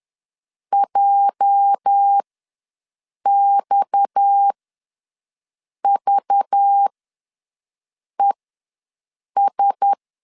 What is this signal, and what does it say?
Morse code, heard as a keyed tone, about 10 wpm, spaced as for 5 wpm, 800 Hz. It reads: JXVES
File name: LEZIONI / 8 YJ45 / 8 YJ4 ws 10wpm 4020